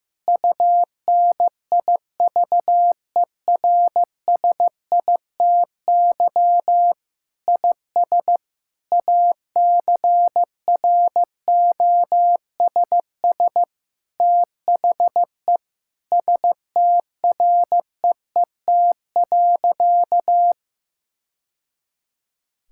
Morse code, heard as a keyed tone, about 15 wpm, 700 Hz. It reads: UNIVERSITY IS ACROSS THE STREET.
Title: Sentences from Top 1000 Words - Review Part 5 - 15wpm